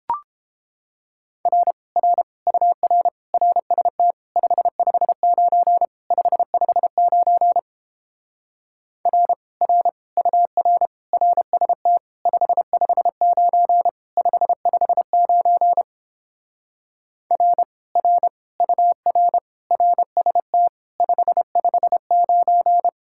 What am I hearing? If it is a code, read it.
Morse code, heard as a keyed tone, about 29 wpm, 700 Hz. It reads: R R UR RST 559 559 R R UR RST 559 559 R R UR RST 559